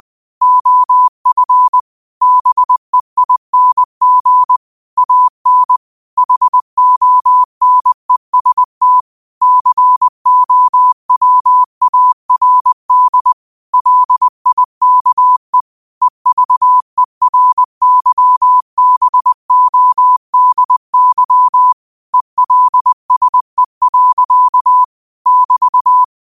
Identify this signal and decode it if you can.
Morse code, heard as a keyed tone, about 20 wpm, 1000 Hz. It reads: OF BEING AN HONEST COWARD LIKE EVERYBODY ELSE. <BT>
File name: qotdfiles / 20wpm